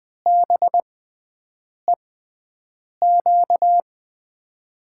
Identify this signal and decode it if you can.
Morse code, heard as a keyed tone, about 20 wpm, spaced as for 7 wpm, 700 Hz. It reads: BEQ
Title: USA Call Signs - 6 Characters - 6x Character Spacing - 20wpm